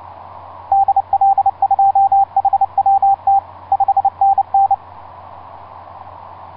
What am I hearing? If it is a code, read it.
Morse code, heard as a keyed tone, about 29 wpm, 790 Hz. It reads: DL2HWT 5NN
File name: s79gm s59z etc 10m cw 30mar2010 1126